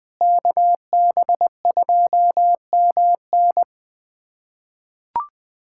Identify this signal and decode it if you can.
Morse code, heard as a keyed tone, about 21 wpm, 700 Hz. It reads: KB2MN
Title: USA Callsign DE Callsign - 5 Characters - 20wpm